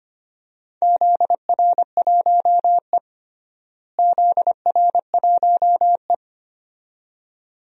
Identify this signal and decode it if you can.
Morse code, heard as a keyed tone, about 25 wpm, 700 Hz. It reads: ZR1E ZR1E
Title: DXCC-100 Call Signs - 4 Characters - Repeated 2x - 25wpm